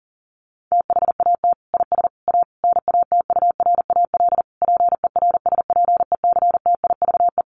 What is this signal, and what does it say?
Morse code, heard as a keyed tone, about 40 wpm, 700 Hz. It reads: THAT IS A NATURAL PERSPECTIVE
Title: Sentences from Top 1200 Words 40wpm